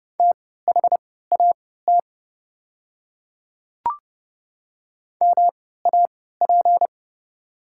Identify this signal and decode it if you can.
Morse code, heard as a keyed tone, about 30 wpm, spaced as for 17 wpm, 690 Hz. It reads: THAT MAP
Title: 3x Character Spacing - Top 500 Words - 30wpm